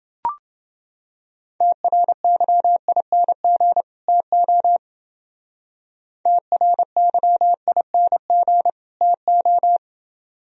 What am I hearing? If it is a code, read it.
Morse code, heard as a keyed tone, about 29 wpm, 700 Hz. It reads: TRYING TO TRYING TO